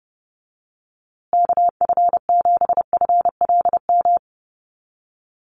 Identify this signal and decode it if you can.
Morse code, heard as a keyed tone, about 30 wpm, 700 Hz. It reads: KF7FLM
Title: USA Call Signs Repeated 3x 30wpm